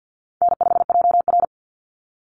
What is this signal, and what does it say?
Morse code, heard as a keyed tone, about 50 wpm, 700 Hz. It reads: N5JR
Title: Call Signs Repeated 2x - 50wpm